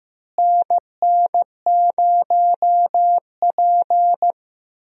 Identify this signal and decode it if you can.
Morse code, heard as a keyed tone, about 15 wpm, 700 Hz. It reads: NN0P
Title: USA Call Signs - 4 Characters - Repeated 3x - 15wpm